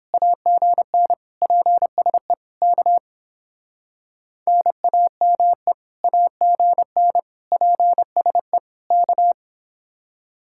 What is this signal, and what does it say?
Morse code, heard as a keyed tone, about 30 wpm, 700 Hz. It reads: AGN PSE K NAME AGN PSE K